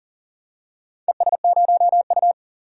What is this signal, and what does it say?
Morse code, heard as a keyed tone, about 40 wpm, 700 Hz. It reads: ES0U
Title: DXCC-100 Call Signs - 4 Characters - 40wpm